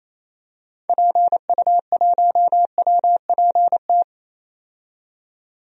Morse code, heard as a keyed tone, about 28 wpm, 700 Hz. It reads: PU1WPT
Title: DXCC-100 Call Signs - 6 Characters - 28wpm